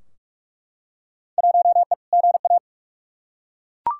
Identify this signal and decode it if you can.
Morse code, heard as a keyed tone, about 47 wpm, 700 Hz. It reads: 1E GA
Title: Field Day Exchanges - 45wpm